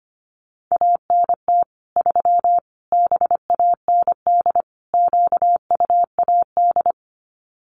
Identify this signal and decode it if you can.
Morse code, heard as a keyed tone, about 25 wpm, 700 Hz. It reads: ANT 3 BAND QUAD
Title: Common QSO Phrases 25wpm